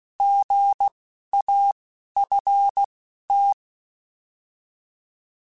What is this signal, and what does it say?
Morse code, heard as a keyed tone, about 16 wpm, 800 Hz. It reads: G A F T